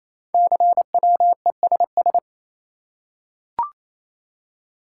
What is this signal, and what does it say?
Morse code, heard as a keyed tone, about 29 wpm, 700 Hz. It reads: CWESS
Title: Code Groups - 5 Letters - 28wpm